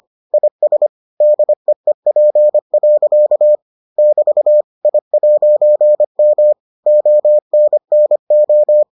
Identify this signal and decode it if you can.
Morse code, heard as a keyed tone, about 25 wpm, 600 Hz. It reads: IS DEEP. <BT> I'M ONNO